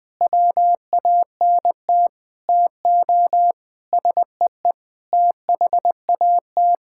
Morse code, heard as a keyed tone, about 21 wpm, 700 Hz. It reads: WANT TO SEE THAT
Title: Sentences with 5-Word Phrases - 20wpm